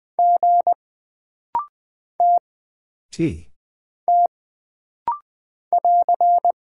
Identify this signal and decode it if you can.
Morse code, heard as a keyed tone, about 20 wpm, 700 Hz. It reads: G T T <AR>
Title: Single Character (Letter:Number:Punctuation:Prosign) - Rapid-Fire - 20wpm